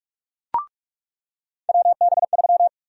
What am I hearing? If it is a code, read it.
Morse code, heard as a keyed tone, about 46 wpm, 700 Hz. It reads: WB3
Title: USA Call Sign Prefixes - 45wpm